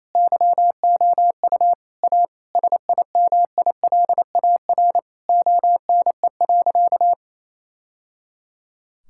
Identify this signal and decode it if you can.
Morse code, heard as a keyed tone, about 28 wpm, 700 Hz. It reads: YOU A SIMILAR ONE.
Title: Sentences from Top 1000 Words - Review Part 5 - 28wpm